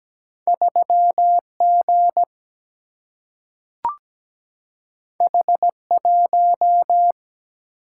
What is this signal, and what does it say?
Morse code, heard as a keyed tone, about 17 wpm, 700 Hz. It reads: 3G H1
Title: Single-Letter Single-Number 17wpm